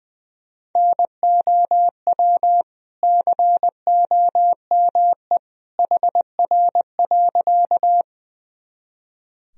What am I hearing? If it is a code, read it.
Morse code, heard as a keyed tone, about 20 wpm, 700 Hz. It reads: NOW COME HR.
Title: Sentences from Top 200 Words and Abbreviations 20wpm